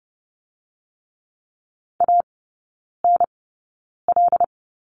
Morse code, heard as a keyed tone, about 30 wpm, spaced as for 9 wpm, 700 Hz. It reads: ANL
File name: course / 30:10wpm / 087 Lesson 15e - Number 5 - Three Characters 30wpm